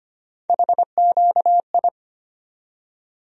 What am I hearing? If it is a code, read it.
Morse code, heard as a keyed tone, about 25 wpm, 700 Hz. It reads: HQI
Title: USA Call Sign Postfixes - 3 Letters 25wpm